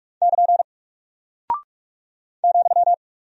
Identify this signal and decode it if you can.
Morse code, heard as a keyed tone, about 46 wpm, 700 Hz. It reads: <KN> ,
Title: Single Character (Letter:Number:Punctuation:Prosign) Alt 45wpm